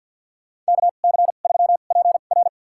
Morse code, heard as a keyed tone, about 52 wpm, spaced as for 37 wpm, 700 Hz. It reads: KC3PR
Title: USA Call Signs - 2x Character Spacing - 50wpm